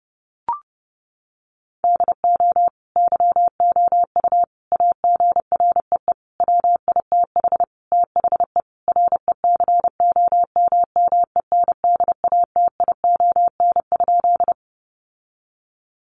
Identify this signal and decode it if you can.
Morse code, heard as a keyed tone, about 30 wpm, 700 Hz. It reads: DO YOU AGREE WITH THE RECOMMENDATION?
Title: Sentences from Top 1400 Words 30wpm